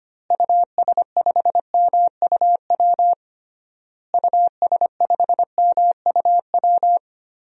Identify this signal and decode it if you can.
Morse code, heard as a keyed tone, about 25 wpm, 700 Hz. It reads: US5MUW US5MUW
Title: DXCC-100 Call Signs - 6 Characters - Repeated 3x - 25wpm